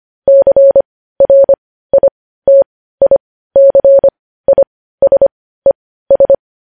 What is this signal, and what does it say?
Morse code, heard as a keyed tone, about 25 wpm, spaced as for 15 wpm, 550 Hz. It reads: CRITICISES
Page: news headlines in morse code at 15 wpm